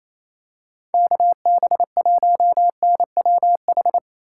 Morse code, heard as a keyed tone, about 28 wpm, 700 Hz. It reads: KB1NWH
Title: USA Call Signs - 6 Characters - 2x Character Spacing - 28wpm